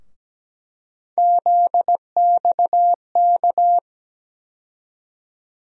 Morse code, heard as a keyed tone, about 17 wpm, 700 Hz. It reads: ZXK
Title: USA Call Sign Postfixes - 3 Letters 17wpm